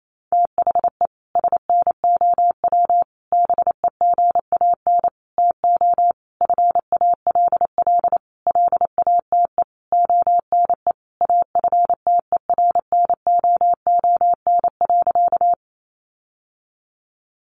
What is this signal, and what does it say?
Morse code, heard as a keyed tone, about 28 wpm, 700 Hz. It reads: THE SNOW BEGAN TO FALL LATE ONE AFTERNOON.
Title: Sentences from Top 1500 Words 28wpm